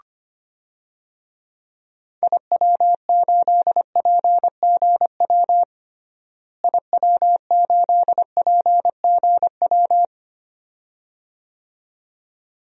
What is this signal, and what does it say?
Morse code, heard as a keyed tone, about 25 wpm, 700 Hz. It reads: IW8PGW IW8PGW
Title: DXCC-100 Call Signs - 6 Characters - Repeated 2x - 25wpm